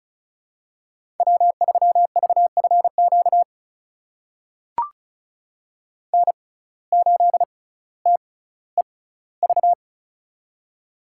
Morse code, heard as a keyed tone, about 36 wpm, spaced as for 18 wpm, 700 Hz. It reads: W3VFQ N8TEV
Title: USA Call Signs - 5 Characters - 6x Character Spacing - 35wpm